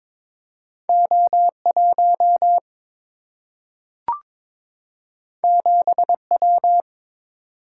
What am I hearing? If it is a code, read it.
Morse code, heard as a keyed tone, about 22 wpm, 700 Hz. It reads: O 1 7 W
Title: Single-Letter Single-Number 22wpm